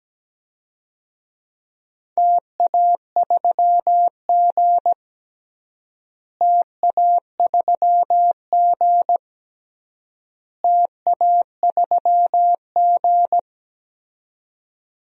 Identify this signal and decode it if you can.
Morse code, heard as a keyed tone, about 17 wpm, 700 Hz. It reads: TA3G TA3G TA3G
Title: DXCC-100 Call Signs - 4 Characters - Repeated 3x - 17wpm